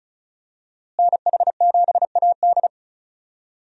Morse code, heard as a keyed tone, about 36 wpm, 700 Hz. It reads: NH7AD
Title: USA Call Signs - 5 Characters - Repeated 3x - 35wpm